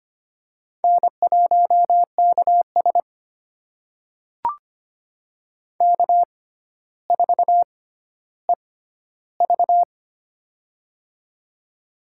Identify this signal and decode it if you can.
Morse code, heard as a keyed tone, about 25 wpm, spaced as for 13 wpm, 700 Hz. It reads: N1KS K4EV
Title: USA Call Signs - 4 Characters - 6x Character Spacing - 25wpm